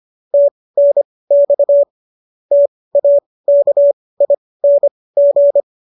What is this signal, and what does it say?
Morse code, heard as a keyed tone, about 25 wpm, spaced as for 18 wpm, 570 Hz. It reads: TNX TAKING
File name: qso223 18